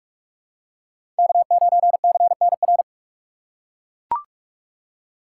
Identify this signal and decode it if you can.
Morse code, heard as a keyed tone, about 43 wpm, 700 Hz. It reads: K9CNR
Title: USA Call Signs - 5 Characters - 2x Character Spacing - 45wpm